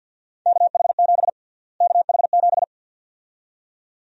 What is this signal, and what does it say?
Morse code, heard as a keyed tone, about 52 wpm, 700 Hz. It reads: KH7 KH7
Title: USA Call Sign Prefixes - Repeated 2x - 50wpm